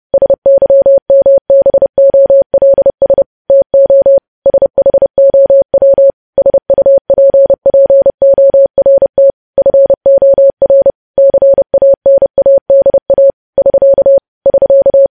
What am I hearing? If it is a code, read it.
Morse code, heard as a keyed tone, about 30 wpm, 550 Hz. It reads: SYMBOLS TO SHOW SUPPORT FOR CANADA <SK> <SK>
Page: news headlines in morse code at 30 wpm